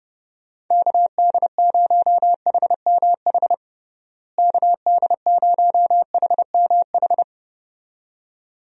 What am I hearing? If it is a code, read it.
Morse code, heard as a keyed tone, about 30 wpm, 700 Hz. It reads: KD0HMH KD0HMH
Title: USA Call Signs - 6 Characters - Repeated 2x - 30wpm